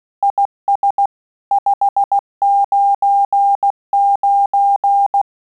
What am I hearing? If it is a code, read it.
Morse code, heard as a keyed tone, about 16 wpm, 800 Hz. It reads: IS 599
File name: QSO CW 75 CPM